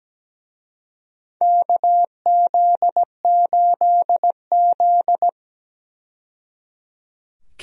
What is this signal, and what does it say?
Morse code, heard as a keyed tone, about 17 wpm, 700 Hz. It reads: KZ8Z
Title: USA Call Signs - 4 Characters - 17wpm